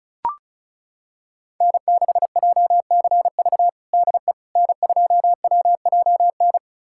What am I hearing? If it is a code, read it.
Morse code, heard as a keyed tone, about 36 wpm, 700 Hz. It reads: N6JCV DE N2WJN
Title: USA Callsign DE Callsign - 5 Characters - 35wpm